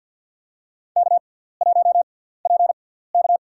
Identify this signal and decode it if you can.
Morse code, heard as a keyed tone, about 53 wpm, spaced as for 17 wpm, 700 Hz. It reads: K1PK
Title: USA Call Signs - 4 Characters - 6x Character Spacing - 50wpm